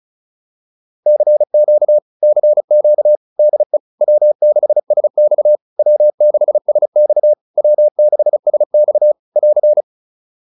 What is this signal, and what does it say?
Morse code, heard as a keyed tone, about 34 wpm, 600 Hz. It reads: CQ CQ DE W6SX W6SX W6SX <AR>
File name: QSO102 35